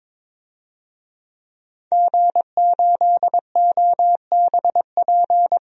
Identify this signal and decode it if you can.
Morse code, heard as a keyed tone, about 22 wpm, 700 Hz. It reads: G8OBP
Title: DXCC-100 Call Signs - 5 Characters - 22wpm